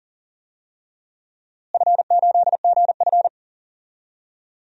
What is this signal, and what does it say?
Morse code, heard as a keyed tone, about 40 wpm, 700 Hz. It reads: F8GF
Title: DXCC-100 Call Signs - 4 Characters - Repeated 2x - 40wpm